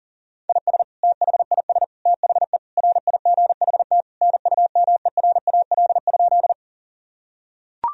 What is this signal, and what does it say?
Morse code, heard as a keyed tone, about 40 wpm, 700 Hz. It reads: IS THIS THE RIGHT NUMERAL?
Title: Sentences from Top 1000 Words - Review Part 2 - 40wpm